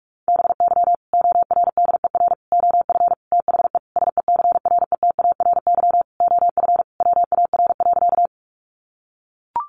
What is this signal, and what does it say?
Morse code, heard as a keyed tone, about 44 wpm, 700 Hz. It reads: BY ORDER OF THE SECRETARY OF WAR.
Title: Sentences from Top 1300 Words 45wpm